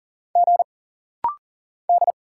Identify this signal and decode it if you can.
Morse code, heard as a keyed tone, about 40 wpm, 700 Hz. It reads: G D
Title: Single Letters - Rapid-Fire 40wpm